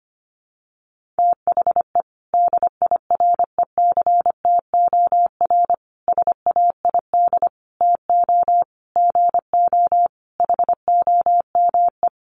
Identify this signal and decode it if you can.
Morse code, heard as a keyed tone, about 25 wpm, 700 Hz. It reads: THE DIRECTOR SAID TO GO HOME
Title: Sentences from Top 1300 Words 25wpm